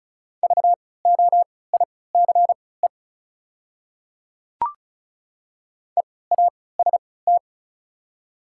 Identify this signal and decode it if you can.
Morse code, heard as a keyed tone, about 34 wpm, spaced as for 20 wpm, 700 Hz. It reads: VOICE EAST
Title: 3x Character Spacing - Top 500 Words - 35wpm